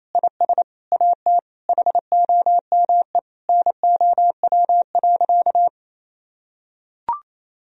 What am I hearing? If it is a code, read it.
Morse code, heard as a keyed tone, about 27 wpm, 700 Hz. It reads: IS AT HOME NOW.
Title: Sentences from Top 1000 Words - Review Part 3 - 28wpm